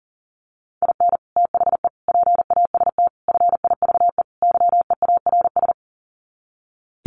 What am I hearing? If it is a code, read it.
Morse code, heard as a keyed tone, about 40 wpm, 700 Hz. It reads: IN THE PAST FIVE YEARS
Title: Sets of 5 Words Encore 40wpm